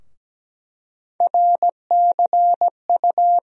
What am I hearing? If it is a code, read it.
Morse code, heard as a keyed tone, about 17 wpm, 700 Hz. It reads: RCU
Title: USA Call Sign Postfixes - 3 Letters 17wpm